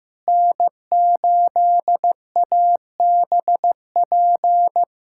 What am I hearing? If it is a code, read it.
Morse code, heard as a keyed tone, about 15 wpm, 700 Hz. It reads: N8ABP
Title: USA Callsign DE Callsign - 5 Characters - 15wpm